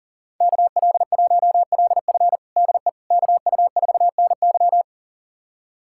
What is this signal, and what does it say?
Morse code, heard as a keyed tone, about 37 wpm, 700 Hz. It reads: KL1LF DE KU4NY
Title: USA Callsign DE Callsign - 5 Characters - 40wpm